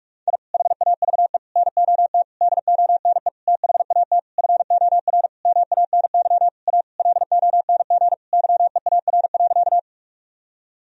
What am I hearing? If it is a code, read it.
Morse code, heard as a keyed tone, about 47 wpm, 700 Hz. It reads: I HAVE NOT DONE THAT FOR MANY A LONG YEAR.
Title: Sentences from Top 1000 Words - Review Part 1 - 45wpm